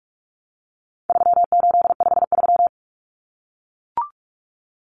Morse code, heard as a keyed tone, about 44 wpm, 700 Hz. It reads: 3853
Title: Common Cut Numbers - 4-Digit 45wpm